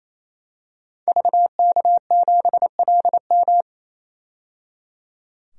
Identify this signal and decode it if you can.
Morse code, heard as a keyed tone, about 28 wpm, 700 Hz. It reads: VK7LM
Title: DXCC-100 Call Signs - 5 Characters - 28wpm